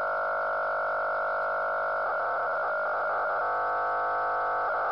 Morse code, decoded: none